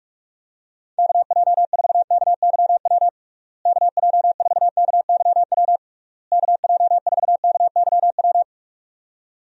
Morse code, heard as a keyed tone, about 46 wpm, 700 Hz. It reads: KJ4KYW KJ4KYW KJ4KYW
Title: USA Call Signs - 6 Characters - Repeated 3x - 45wpm